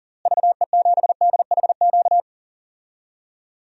Morse code, heard as a keyed tone, about 37 wpm, 700 Hz. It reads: VE7DHQ
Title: DXCC-100 Call Signs - 6 Characters - Repeated 3x - 40wpm